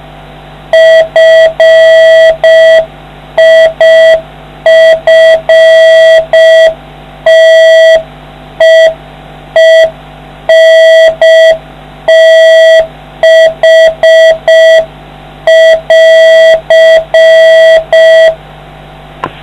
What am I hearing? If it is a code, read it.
Morse code, heard as a keyed tone, about 5 wpm, 650 Hz, with noise. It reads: FIFTEENTH <AR>